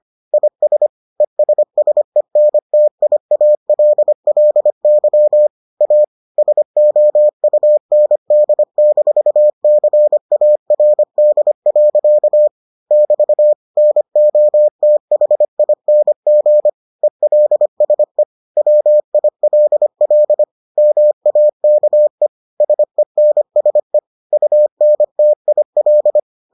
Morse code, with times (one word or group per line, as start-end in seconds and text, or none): 0.3-0.9 IS
1.2-5.5 ESSENTIALLY
5.8-6.0 A
6.4-12.5 SOUND-CARD.
12.9-13.5 <BT>
13.8-16.7 NOTHING
17.0-18.2 ELSE
18.6-20.4 WILL
20.8-22.3 MAKE
22.6-24.0 SENSE
24.3-26.2 UNTIL